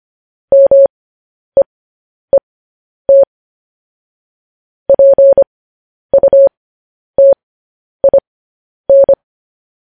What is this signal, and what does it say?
Morse code, heard as a keyed tone, about 25 wpm, spaced as for 10 wpm, 550 Hz. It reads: MEET PUTIN
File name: podcast / News headlines-1x-25x10